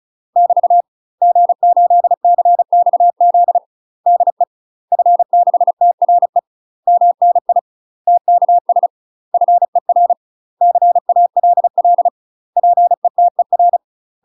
Morse code, heard as a keyed tone, about 34 wpm, 700 Hz. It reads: <BT> G8CXZ DE F6TRE MNI TKS FER CALL PETER
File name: UK MASTER QSOs TRAINING / 35WPM / UK MASTER QSOs 0040